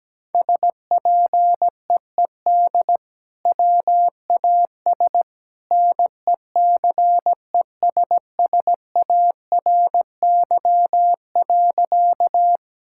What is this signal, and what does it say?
Morse code, heard as a keyed tone, about 17 wpm, 700 Hz. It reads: SPEED WAS NECESSARY.
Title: Sentences from Top 1000 Words - Review Part 5 - 17wpm